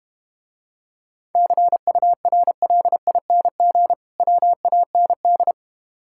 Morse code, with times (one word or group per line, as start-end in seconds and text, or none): 1.4-3.9 CURLING
4.2-5.5 WAND